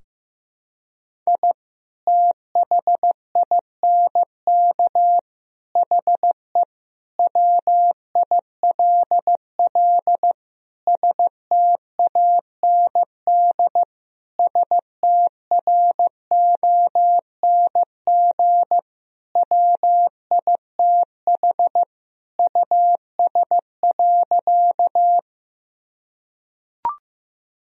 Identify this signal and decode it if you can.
Morse code, heard as a keyed tone, about 16 wpm, 700 Hz. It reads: I THINK HE WILL STAND STRONG WITH US.
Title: Sentences from Top 1000 Words - Review Part 3 - 15wpm